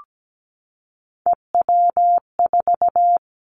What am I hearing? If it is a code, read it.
Morse code, heard as a keyed tone, about 17 wpm, 700 Hz. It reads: EW4
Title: Callsign Prefixes for Top 100 DXCC - 17wpm